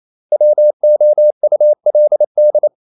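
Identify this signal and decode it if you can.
Morse code, heard as a keyed tone, about 28 wpm, 600 Hz. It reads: WOULD
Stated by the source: Morse code practice at 28 wpm, text of Ken Follett's novel Never in English, from CW med Gnister.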